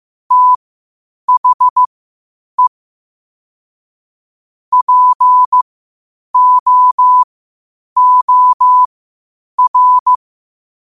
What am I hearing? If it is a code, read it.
Morse code, heard as a keyed tone, about 15 wpm, spaced as for 8 wpm, 1000 Hz. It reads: THE POOR
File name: qotdfiles / 8wpm